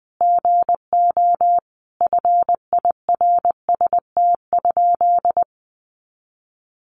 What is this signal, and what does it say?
Morse code, heard as a keyed tone, about 20 wpm, 700 Hz. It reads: GO FIRST?
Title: Sentences from Top 100 Words 20wpm